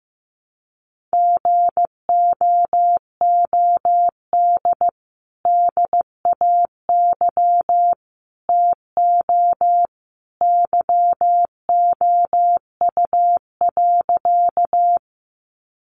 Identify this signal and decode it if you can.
Morse code, heard as a keyed tone, about 15 wpm, 700 Hz. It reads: GOOD DAY TO YOU.
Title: Sentences from Top 200 Words 15wpm